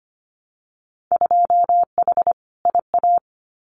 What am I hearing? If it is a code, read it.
Morse code, heard as a keyed tone, about 25 wpm, 700 Hz. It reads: 2H IA
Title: Winter Field Day Exchanges Encore 25wpm